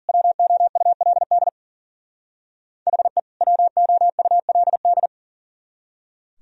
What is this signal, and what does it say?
Morse code, heard as a keyed tone, about 49 wpm, 700 Hz. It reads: WOULD HE WOULD
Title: Speed-Racing - Sets of 2 Words 40wpm